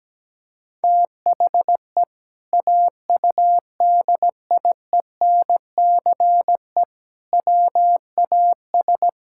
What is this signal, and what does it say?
Morse code, heard as a keyed tone, about 17 wpm, 700 Hz. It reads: THE AUDIENCE WAS
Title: Sentences from Top 1000 Words - Review Part 5 - 17wpm